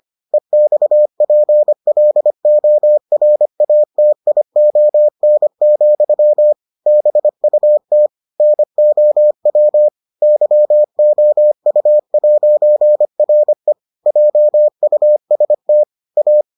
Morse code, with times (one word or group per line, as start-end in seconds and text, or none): 0.3-6.5 EXPLORATION,
6.9-8.1 BUT
8.4-9.9 NOW
10.2-13.7 YOU'RE
14.1-15.8 JUST
16.2-16.4 A